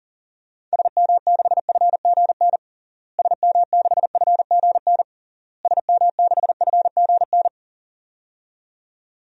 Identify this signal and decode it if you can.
Morse code, heard as a keyed tone, about 40 wpm, 700 Hz. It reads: SM6FGN SM6FGN SM6FGN
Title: DXCC-100 Call Signs - 6 Characters - Repeated 3x - 40wpm